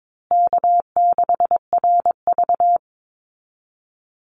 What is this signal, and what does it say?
Morse code, heard as a keyed tone, about 22 wpm, 700 Hz. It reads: K6RV
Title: Easy Call Signs Repeated 2x - 22wpm